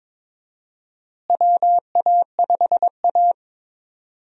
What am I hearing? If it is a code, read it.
Morse code, heard as a keyed tone, about 21 wpm, 700 Hz. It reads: WA5A
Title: USA Call Signs - 4 Characters - Repeated 2x - 22wpm